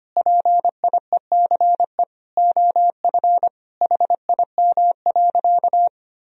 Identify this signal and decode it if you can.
Morse code, heard as a keyed tone, about 25 wpm, 700 Hz. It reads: PIECE OF HIM.
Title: Sentences from Top 1000 Words - Review Part 2 - 25wpm